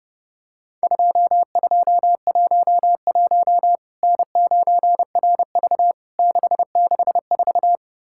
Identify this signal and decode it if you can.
Morse code, heard as a keyed tone, about 29 wpm, 700 Hz. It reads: 2211 N9RV 664
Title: Worked All Europe DX Contest - QTC Exchange - 30wpm